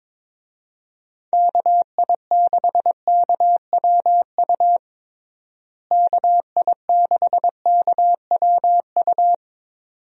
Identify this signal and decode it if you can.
Morse code, heard as a keyed tone, about 22 wpm, 700 Hz. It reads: KI6KWU KI6KWU
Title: USA Call Signs - 6 Characters - Repeated 2x - 22wpm